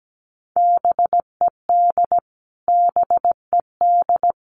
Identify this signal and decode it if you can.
Morse code, heard as a keyed tone, about 17 wpm, 700 Hz. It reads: BED BED
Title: Top 500 Words Repeated 17wpm